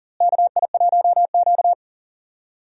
Morse code, heard as a keyed tone, about 40 wpm, 700 Hz. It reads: KI1Q
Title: USA Call Signs - 4 Characters - Repeated 2x - 40wpm